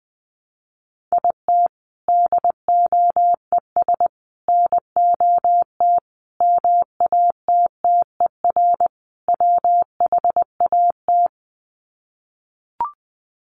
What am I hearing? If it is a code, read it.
Morse code, heard as a keyed tone, about 20 wpm, 700 Hz. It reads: IT DOES NOT MATTER WHAT
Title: Sets of 5 Words Encore 20wpm